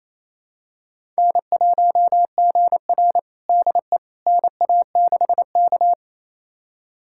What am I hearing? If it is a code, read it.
Morse code, heard as a keyed tone, about 28 wpm, 700 Hz. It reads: N1GR DE NA6K
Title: USA Callsign DE Callsign - 4 Characters - 28wpm